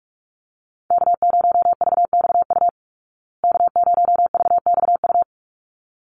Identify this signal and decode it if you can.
Morse code, heard as a keyed tone, about 44 wpm, 700 Hz. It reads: K0VXU K0VXU
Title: Call Signs Repeated 3x - 45wpm